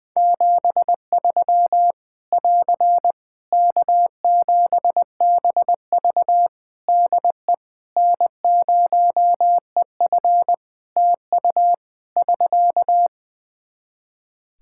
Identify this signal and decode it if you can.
Morse code, heard as a keyed tone, about 20 wpm, 700 Hz. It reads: 73 <AR> K7BV DE N0EF TU <SK>
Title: Practice QSO - Exchange 4 - Ending - 20wpm